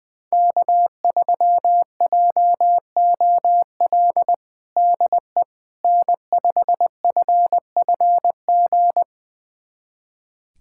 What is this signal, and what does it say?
Morse code, heard as a keyed tone, about 21 wpm, 700 Hz. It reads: K3JOL DE N5FFG
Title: USA Callsign DE Callsign - 5 Characters - 20wpm